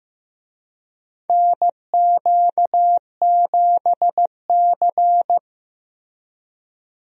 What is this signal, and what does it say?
Morse code, heard as a keyed tone, about 16 wpm, 700 Hz. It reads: NQ7C
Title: USA Call Signs - 4 Characters - Repeated 3x - 15wpm